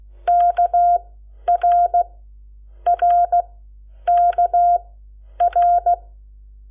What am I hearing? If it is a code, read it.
Morse code, heard as a keyed tone, about 16 wpm, spaced as for 9 wpm, 700 Hz. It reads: KRRKR
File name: Lecciones / CLASE Nº7